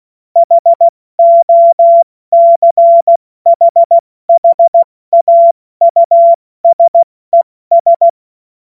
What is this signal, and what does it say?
Morse code, heard as a keyed tone, about 16 wpm, spaced as for 14 wpm, 680 Hz. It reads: HOCHHAUSES
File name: Morsekurs / Witze / Witz 02 April 2024 70 F80